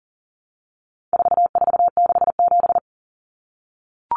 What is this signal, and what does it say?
Morse code, heard as a keyed tone, about 40 wpm, 700 Hz. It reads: 4467
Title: Numbers - 4-Digit 40wpm